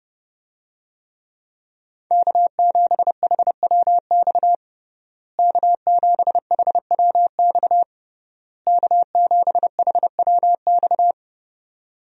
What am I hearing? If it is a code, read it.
Morse code, heard as a keyed tone, about 30 wpm, 700 Hz. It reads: K7HWX K7HWX K7HWX
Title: USA Call Signs - 5 Characters - Repeated 3x - 30wpm